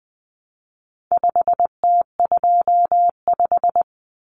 Morse code, heard as a keyed tone, about 20 wpm, 700 Hz. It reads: 5T25
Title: Common Cut Numbers - 4-Digit 20wpm